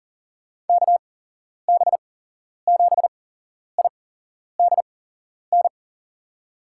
Morse code, heard as a keyed tone, about 40 wpm, spaced as for 11 wpm, 700 Hz. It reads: KB7IDN
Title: USA Call Signs - 6 Characters - 8x Character Spacing - 40wpm